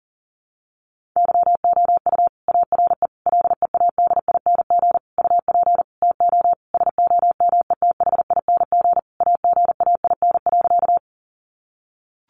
Morse code, heard as a keyed tone, about 40 wpm, 700 Hz. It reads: YOU ARE LEADING UP TO SOMETHING AGAIN.
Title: Sentences from Top 1100 Words 40wpm